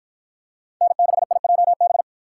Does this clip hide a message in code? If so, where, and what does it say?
Yes: Morse code from 0.8 to 2.0 s, heard as a keyed tone, about 52 wpm, 700 Hz. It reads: N6IJB